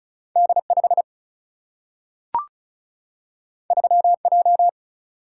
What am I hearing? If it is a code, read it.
Morse code, heard as a keyed tone, about 36 wpm, 700 Hz. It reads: D5 3J
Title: Single-Letter Single-Number 35wpm